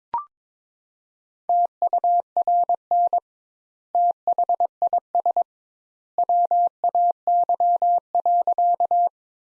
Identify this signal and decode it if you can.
Morse code, heard as a keyed tone, about 22 wpm, 700 Hz. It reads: TURN THIS WAY.
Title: Sentences Top 200 Words - 22wpm